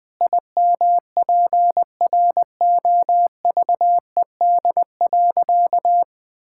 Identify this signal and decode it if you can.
Morse code, heard as a keyed tone, about 20 wpm, 700 Hz. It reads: IMPROVED.
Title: Sentences with 4-Word Phrases - 20wpm